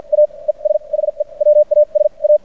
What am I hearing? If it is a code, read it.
Morse code, heard as a keyed tone, about 56 wpm, spaced as for 39 wpm, 590 Hz. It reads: AES HE WAS A
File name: archived / CRSnet